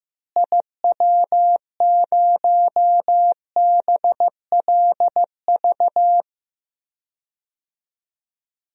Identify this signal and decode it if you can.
Morse code, heard as a keyed tone, about 15 wpm, 700 Hz. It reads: IW0BLV
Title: DXCC-100 Call Signs - 6 Characters - Repeated 2x - 15wpm